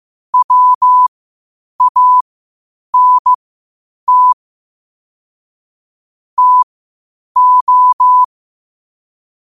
Morse code, heard as a keyed tone, about 15 wpm, spaced as for 8 wpm, 1000 Hz. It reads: WANT TO
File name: qotdfiles / 8wpm